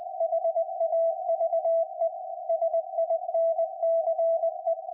Morse code, heard as a keyed tone, about 20 wpm, 660 Hz. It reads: HAVE SINCE
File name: IC-7100